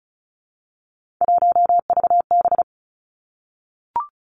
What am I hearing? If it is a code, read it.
Morse code, heard as a keyed tone, about 36 wpm, 700 Hz. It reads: 1VB
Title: Three Challenging Letter-Number Combinations 35wpm